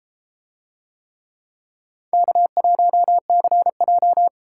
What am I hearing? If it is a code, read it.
Morse code, heard as a keyed tone, about 30 wpm, 700 Hz. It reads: K1CJ